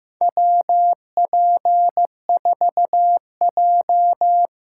Morse code, heard as a keyed tone, about 15 wpm, 700 Hz. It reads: WP4J
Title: USA Call Signs - 6 Characters - 8x Character Spacing - 15wpm